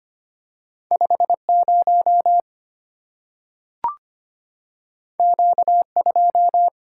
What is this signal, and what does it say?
Morse code, heard as a keyed tone, about 25 wpm, 700 Hz. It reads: 50 Q2
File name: Two Letter-Number Combinations - 25wpm